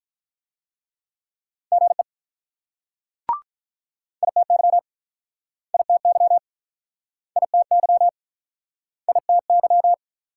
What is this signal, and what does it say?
Morse code, heard as a keyed tone, about 48 wpm, 700 Hz. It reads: GE ITY ITY ITY ITY